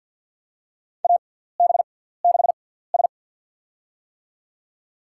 Morse code, heard as a keyed tone, about 50 wpm, spaced as for 17 wpm, 700 Hz. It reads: AB6S
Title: USA Call Signs - 4 Characters - 6x Character Spacing - 50wpm